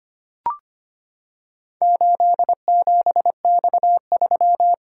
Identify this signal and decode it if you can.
Morse code, heard as a keyed tone, about 25 wpm, 700 Hz. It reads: 87X3
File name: practice / 25wpm / Four Letter-Number Combinations - 25wpm